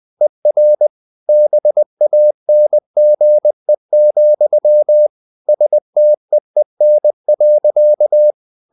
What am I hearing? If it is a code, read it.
Morse code, heard as a keyed tone, about 20 wpm, 600 Hz. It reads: ER BANGE, STEEN.
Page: CW med Gnister